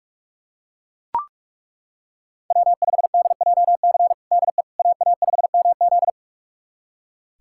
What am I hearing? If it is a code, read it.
Morse code, heard as a keyed tone, about 43 wpm, 700 Hz. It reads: W5DJC DE AA5MZ
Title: USA Callsign DE Callsign - 5 Characters - 45wpm